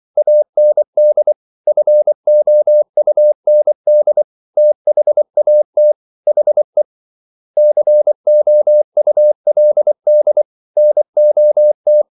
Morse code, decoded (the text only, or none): AND FOUND THAT HE COULD NOT